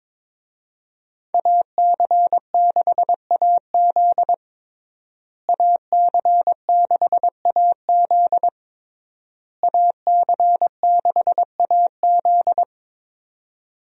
Morse code, heard as a keyed tone, about 22 wpm, 700 Hz. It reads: AC6AZ AC6AZ AC6AZ
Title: USA Call Signs - 5 Characters - Repeated 3x - 22wpm